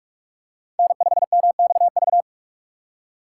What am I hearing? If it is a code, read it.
Morse code, heard as a keyed tone, about 45 wpm, 700 Hz. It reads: N5MXV